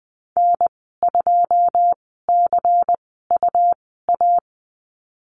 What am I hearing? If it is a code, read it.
Morse code, heard as a keyed tone, about 20 wpm, spaced as for 14 wpm, 700 Hz. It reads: N2CUA
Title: 173 Lesson 28g - Number 9 - Call Signs 20wpm